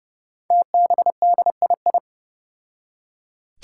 Morse code, heard as a keyed tone, about 30 wpm, 700 Hz. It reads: TBDII
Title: Five Challenging Letter-Number Combinations 30wpm